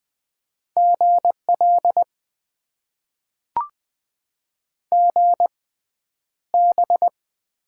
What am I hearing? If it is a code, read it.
Morse code, heard as a keyed tone, about 20 wpm, spaced as for 10 wpm, 700 Hz. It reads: GL GB
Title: Top 90 QSO Elements - 6x Character Spacing - 20wpm